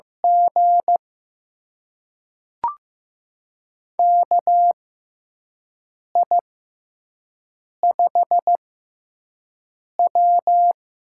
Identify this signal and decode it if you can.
Morse code, heard as a keyed tone, about 15 wpm, spaced as for 5 wpm, 700 Hz. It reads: G KI5W
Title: USA Call Signs - 4 Characters - 6x Character Spacing - 15wpm